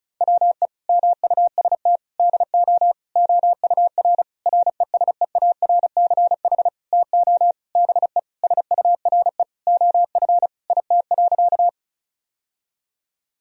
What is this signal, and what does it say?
Morse code, heard as a keyed tone, about 36 wpm, 700 Hz. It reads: WE MUST DO OUR RESEARCH TO BE SURE OF IT.
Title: Sentences from Top 1000 Words - Review Part 4 - 35wpm